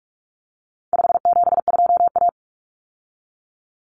Morse code, heard as a keyed tone, about 44 wpm, 700 Hz. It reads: 572A
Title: Common Cut Numbers - 4-Digit 45wpm